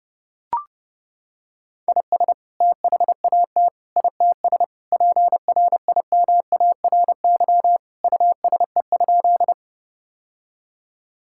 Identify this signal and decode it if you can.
Morse code, heard as a keyed tone, about 30 wpm, 700 Hz. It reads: IS THAT ITS PRIMARY USE?
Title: Sentences from Top 1000 Words - Review Part 5 - 30wpm